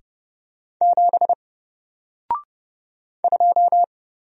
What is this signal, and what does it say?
Morse code, heard as a keyed tone, about 30 wpm, 700 Hz. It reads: 7 2
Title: Single Character (Letter:Number:Punctuation:Prosign) Alt 30wpm